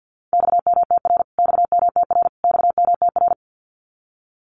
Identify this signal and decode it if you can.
Morse code, heard as a keyed tone, about 50 wpm, 700 Hz. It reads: XMTR XMTR XMTR
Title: Abbreviations Repeated 3x - 50wpm